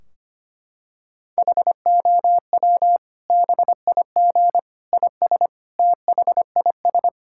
Morse code, heard as a keyed tone, about 25 wpm, 700 Hz. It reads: HOW BIG IS THIS
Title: Sentences from Top 1600 Words 25wpm